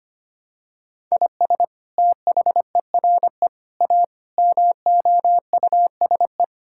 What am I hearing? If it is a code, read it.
Morse code, heard as a keyed tone, about 25 wpm, 700 Hz. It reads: IS THERE A MOUSE
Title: Sentences from Top 1000 Words - Review Part 5 - 25wpm